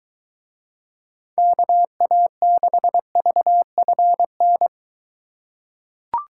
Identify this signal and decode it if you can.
Morse code, heard as a keyed tone, about 23 wpm, 700 Hz. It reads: KA6VFN